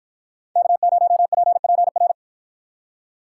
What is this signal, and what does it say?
Morse code, heard as a keyed tone, about 53 wpm, 700 Hz. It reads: K0PPR